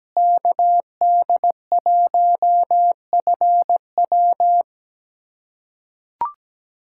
Morse code, heard as a keyed tone, about 17 wpm, 700 Hz. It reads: KD1FW
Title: USA Callsign DE Callsign 17wpm